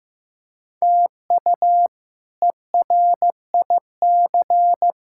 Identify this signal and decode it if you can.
Morse code, heard as a keyed tone, about 15 wpm, 700 Hz. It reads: TU ERIC
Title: K1USN SST QSO Practice - 15wpm